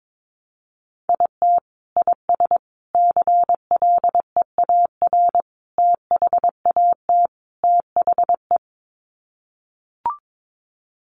Morse code, heard as a keyed tone, about 22 wpm, 700 Hz. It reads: IT IS CLEAR THAT THE
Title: Sets of 5 Words Encore 22wpm